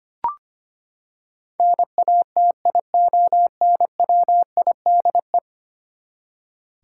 Morse code, heard as a keyed tone, about 25 wpm, 700 Hz. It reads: NATIONWIDE
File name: Ten Letter Words - 25wpm